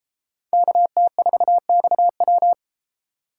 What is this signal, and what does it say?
Morse code, heard as a keyed tone, about 31 wpm, 700 Hz. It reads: KT4XW